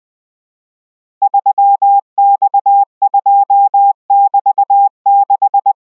Morse code, heard as a keyed tone, about 20 wpm, 800 Hz. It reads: 3X2<BT>6